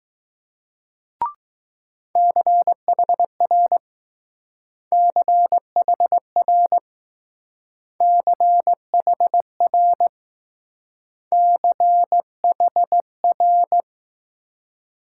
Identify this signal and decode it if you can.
Morse code, heard as a keyed tone, about 20 wpm, 700 Hz. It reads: CHR CHR CHR CHR